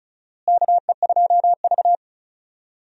Morse code, heard as a keyed tone, about 36 wpm, 700 Hz. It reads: KE2V
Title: USA Call Signs - 4 Characters - Repeated 2x - 35wpm